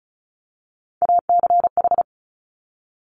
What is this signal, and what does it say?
Morse code, heard as a keyed tone, about 36 wpm, 700 Hz. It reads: ACH
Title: Top Three Letter Combinations in Words 35wpm